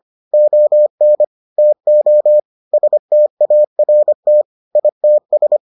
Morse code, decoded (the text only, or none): ON TO START ITS